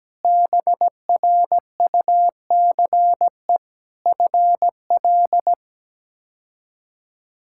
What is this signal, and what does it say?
Morse code, heard as a keyed tone, about 17 wpm, 700 Hz. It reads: BRUCE FL
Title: K1USN SST QSO Practice - 17wpm